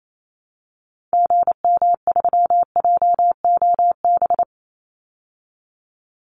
Morse code, heard as a keyed tone, about 28 wpm, 700 Hz. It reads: GM3JOB
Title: Call Signs Repeated 2x - 28wpm